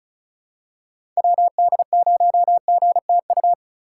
Morse code, heard as a keyed tone, about 36 wpm, 690 Hz. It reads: WD0GTU
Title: USA Call Signs - 3x Character Spacing - 35wpm